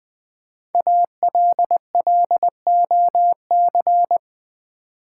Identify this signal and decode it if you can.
Morse code, heard as a keyed tone, about 20 wpm, 700 Hz. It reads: ALLOC
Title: Build A Word - 5 Letters At A Time - 10-Letter Words - 20wpm